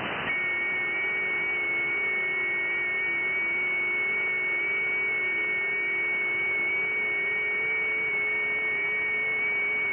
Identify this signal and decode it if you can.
FT8.
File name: IC-7100